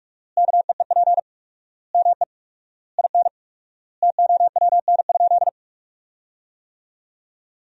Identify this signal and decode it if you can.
Morse code, heard as a keyed tone, about 46 wpm, 700 Hz. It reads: KEEP ME IN TOWN?
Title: 4x Word-Spacing - Sentences from Top 500 Words - 45wpm